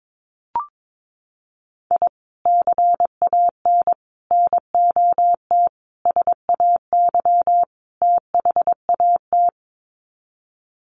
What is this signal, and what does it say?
Morse code, heard as a keyed tone, about 22 wpm, 700 Hz. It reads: I CAN NOT SAY THAT
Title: Sets of 5 Words Encore2 22wpm